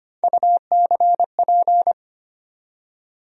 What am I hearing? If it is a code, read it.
Morse code, heard as a keyed tone, about 25 wpm, 700 Hz. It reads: UCP
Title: USA Call Sign Postfixes - 3 Letters 25wpm